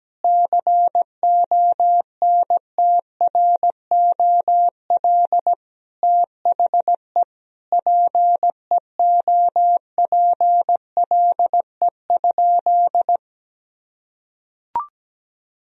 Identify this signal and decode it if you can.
Morse code, heard as a keyed tone, about 17 wpm, 700 Hz. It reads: CONTROL THE PEOPLE?
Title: Sentences from Top 1000 Words - Review Part 3 - 17wpm